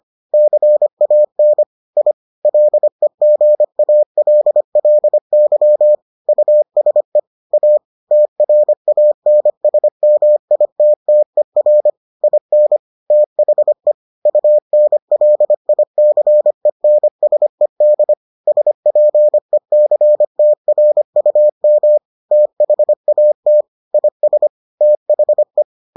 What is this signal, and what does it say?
Morse code, heard as a keyed tone, about 25 wpm, 600 Hz. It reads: CAN I LEGALLY USE A TRANSMITTER IN THE UNLICENSED SPECTRUM THAT IS THE